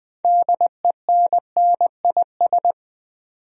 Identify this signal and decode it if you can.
Morse code, heard as a keyed tone, about 20 wpm, 700 Hz. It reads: DENNIS
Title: USA Common Names - 20wpm